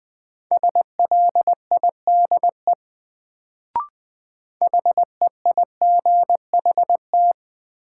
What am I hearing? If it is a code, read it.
Morse code, heard as a keyed tone, about 20 wpm, 700 Hz. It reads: SLIDE HEIGHT
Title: Sentences from Top 1600 Words - New Word Drill Alt - 20wpm